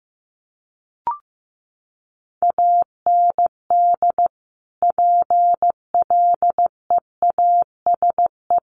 Morse code, heard as a keyed tone, about 15 wpm, 700 Hz. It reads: AND PLEASE
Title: Sentences from Top 1500 Words 15wpm